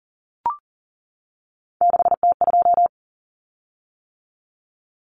Morse code, heard as a keyed tone, about 40 wpm, 700 Hz. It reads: 6T2
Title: Common Cut Numbers - 3-Digit 40wpm